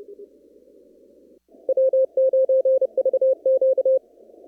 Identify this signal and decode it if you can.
Morse code, heard as a keyed tone, about 30 wpm, 530 Hz. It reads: W9VQ
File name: CW / kt5j wpx cw-op-k5pi